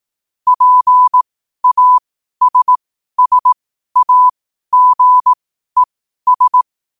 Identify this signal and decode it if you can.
Morse code, heard as a keyed tone, about 18 wpm, spaced as for 12 wpm, 1000 Hz. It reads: PASSAGES
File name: qotdfiles / 12wpm